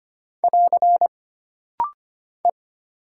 Morse code, heard as a keyed tone, about 25 wpm, 700 Hz. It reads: <AR> E
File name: Single Character (Letter:Number:Punctuation:Prosign) - Rapid-Fire - 25wpm